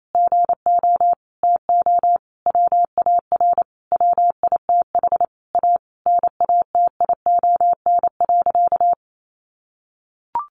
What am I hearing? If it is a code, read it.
Morse code, heard as a keyed tone, about 28 wpm, 700 Hz. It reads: GO TO WAR WITH A NATION.
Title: Sentences from Top 1200 Words 28wpm